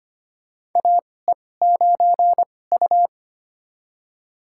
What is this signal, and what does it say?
Morse code, heard as a keyed tone, about 25 wpm, spaced as for 18 wpm, 700 Hz. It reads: AE9U
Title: USA Call Signs - 4 Characters - 2x Character Spacing - 25wpm